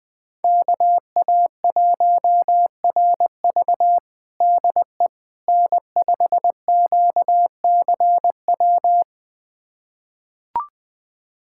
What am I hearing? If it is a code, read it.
Morse code, heard as a keyed tone, about 20 wpm, 700 Hz. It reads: KA1RV DE N5QCW
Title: USA Callsign DE Callsign - 5 Characters - 20wpm